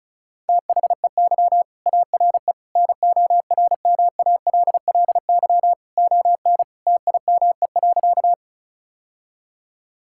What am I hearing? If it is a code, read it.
Morse code, heard as a keyed tone, about 36 wpm, 700 Hz. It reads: THEY ARE NORMALLY ON TIME.
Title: Sentences from Top 1000 Words - Review Part 6 - 35wpm